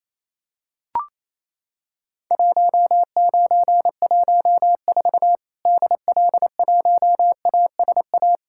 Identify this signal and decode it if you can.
Morse code, heard as a keyed tone, about 28 wpm, 700 Hz. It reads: 1914 DL1ASA
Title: Worked All Europe DX Contest - QTC Exchange - 28wpm